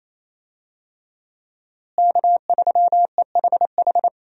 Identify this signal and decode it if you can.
Morse code, heard as a keyed tone, about 27 wpm, 700 Hz. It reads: K3EHH
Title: USA Call Signs - 5 Characters - Repeated 3x - 28wpm